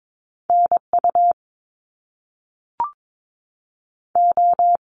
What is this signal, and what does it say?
Morse code, heard as a keyed tone, about 22 wpm, 700 Hz. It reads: NU O
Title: Canadian Provinces and Territories 22wpm